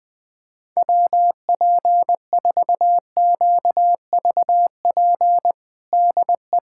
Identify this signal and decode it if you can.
Morse code, heard as a keyed tone, about 20 wpm, 700 Hz. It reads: WP4QVP DE
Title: USA Callsign DE Callsign 20wpm